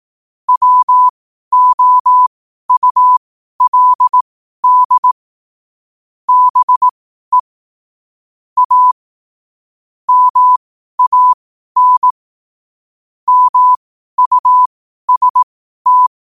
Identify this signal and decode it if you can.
Morse code, heard as a keyed tone, about 18 wpm, spaced as for 12 wpm, 1000 Hz. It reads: WOULD BE A MAN MUST
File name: qotdfiles / 12wpm